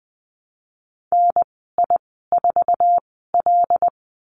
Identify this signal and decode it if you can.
Morse code, heard as a keyed tone, about 20 wpm, spaced as for 15 wpm, 700 Hz. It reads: NI4L
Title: 076 Lesson 13g - Letter L - Call Signs 20wpm